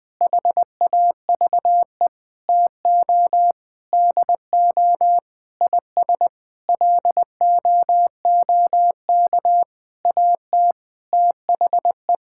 Morse code, heard as a keyed tone, about 20 wpm, 700 Hz. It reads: HAVE TO DO IS LOOK AT THE
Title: Sentences with 9-Word Phrases - 20wpm